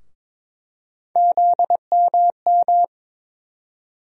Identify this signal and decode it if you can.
Morse code, heard as a keyed tone, about 22 wpm, 700 Hz. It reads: ZMM
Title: USA Call Sign Postfixes - 3 Letters 22wpm